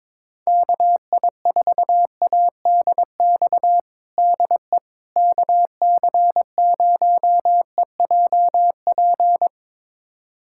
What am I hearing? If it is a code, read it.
Morse code, heard as a keyed tone, about 22 wpm, 700 Hz. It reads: KI4ADX DE KC0EJP
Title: USA Callsign DE Callsign - 6 Characters - 22wpm